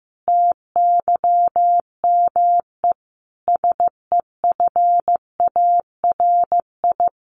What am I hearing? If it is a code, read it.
Morse code, heard as a keyed tone, about 15 wpm, 700 Hz. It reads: TYME SEFARI
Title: A Sound Of Thunder 15wpm